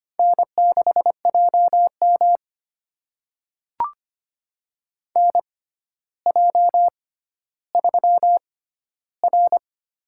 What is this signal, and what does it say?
Morse code, heard as a keyed tone, about 25 wpm, spaced as for 13 wpm, 700 Hz. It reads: N6JM NJ3R
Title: USA Call Signs - 4 Characters - 6x Character Spacing - 25wpm